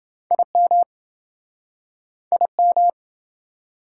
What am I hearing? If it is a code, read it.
Morse code, heard as a keyed tone, about 29 wpm, 700 Hz. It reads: IM IM